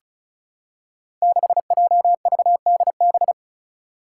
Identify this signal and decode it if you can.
Morse code, heard as a keyed tone, about 36 wpm, 700 Hz. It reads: 6JVDB
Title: Five Challenging Letter-Number Combinations 35wpm